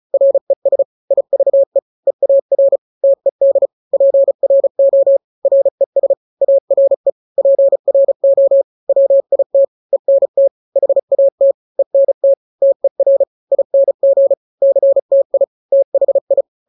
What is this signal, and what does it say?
Morse code, heard as a keyed tone, about 34 wpm, 550 Hz. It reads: RES IVE EAR TED PRO RES ARE PRO WIT ENT HAT ENT TER ING CTI THI
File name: 3-grams 35 30 WPM